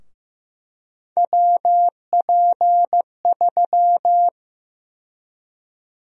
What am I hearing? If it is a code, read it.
Morse code, heard as a keyed tone, about 15 wpm, 700 Hz. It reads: WP3
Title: USA Call Sign Prefixes - Repeated 2x - 15wpm